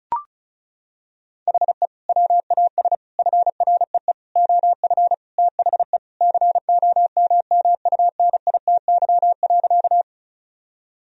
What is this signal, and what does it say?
Morse code, heard as a keyed tone, about 36 wpm, 700 Hz. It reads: HE WAS FREE OF THE COMMUNITY.
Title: Sentences from Top 1000 Words - Review Part 4 - 35wpm